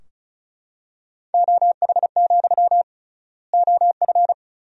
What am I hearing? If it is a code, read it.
Morse code, heard as a keyed tone, about 34 wpm, 700 Hz. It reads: OH, OF
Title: 3x Word-Spacing - Sentences from Top 500 Words - 35wpm